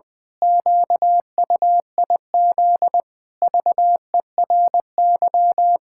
Morse code, heard as a keyed tone, about 21 wpm, 700 Hz. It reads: QUIZ VERY
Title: Pangram Challenge v2 - 20wpm